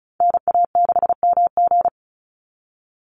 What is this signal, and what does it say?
Morse code, heard as a keyed tone, about 34 wpm, 700 Hz. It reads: NA6MG
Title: NA SOTA Activator Callsigns 35wpm